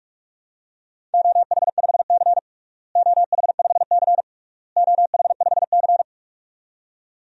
Morse code, heard as a keyed tone, about 46 wpm, 700 Hz. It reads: OH5C OH5C OH5C
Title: DXCC-100 Call Signs - 4 Characters - Repeated 3x - 45wpm